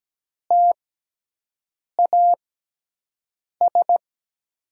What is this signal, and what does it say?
Morse code, heard as a keyed tone, about 17 wpm, spaced as for 6 wpm, 700 Hz. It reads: TAS
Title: USA Call Signs - 5 Characters - 6x Character Spacing - 17wpm